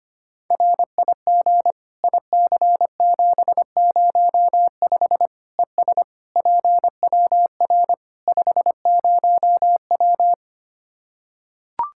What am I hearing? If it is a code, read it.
Morse code, heard as a keyed tone, about 25 wpm, 700 Hz. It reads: RIG IC705 ES PWR 50W
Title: K1USN SST QSO Practice 25wpm